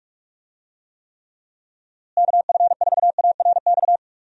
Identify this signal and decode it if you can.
Morse code, heard as a keyed tone, about 46 wpm, 700 Hz. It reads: KF4ARX
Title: USA Call Signs - 6 Characters - Repeated 2x - 45wpm